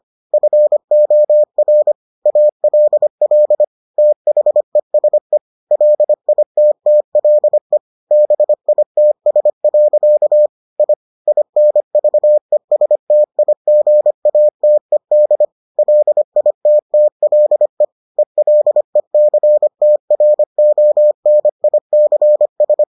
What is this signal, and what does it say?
Morse code, heard as a keyed tone, about 25 wpm, 600 Hz. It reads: FOR ALL THESE LITTLE BITS. I INVESTIGATED LITTLE ELECTRONICS